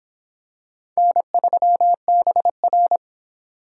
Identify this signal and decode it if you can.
Morse code, heard as a keyed tone, about 22 wpm, 700 Hz. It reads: N3BR